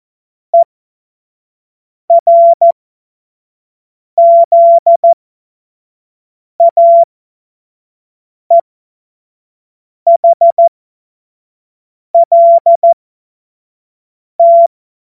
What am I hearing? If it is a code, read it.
Morse code, heard as a keyed tone, about 14 wpm, spaced as for 5 wpm, 680 Hz. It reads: ERZAEHLT
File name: Morsekurs / Witze / Witz 18. Januar 2022, 25 F70